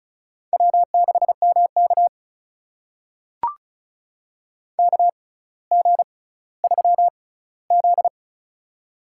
Morse code, heard as a keyed tone, about 36 wpm, spaced as for 18 wpm, 700 Hz. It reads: W6MK KG3Z
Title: USA Call Signs - 4 Characters - 6x Character Spacing - 35wpm